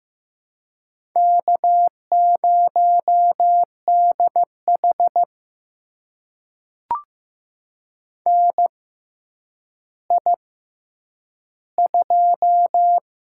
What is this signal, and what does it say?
Morse code, heard as a keyed tone, about 15 wpm, spaced as for 9 wpm, 700 Hz. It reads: K0DH NI2
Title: USA Call Signs - 4 Characters - 6x Character Spacing - 15wpm